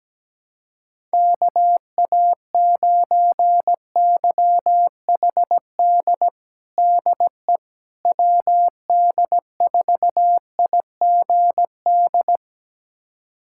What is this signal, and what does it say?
Morse code, heard as a keyed tone, about 17 wpm, 700 Hz. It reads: KA9YHD DE WD4IGD
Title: USA Callsign DE Callsign 17wpm